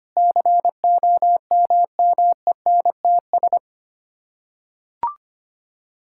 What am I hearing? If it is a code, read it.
Morse code, heard as a keyed tone, about 25 wpm, 700 Hz. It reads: COMMENTS
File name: Sets of 9 Words - 25wpm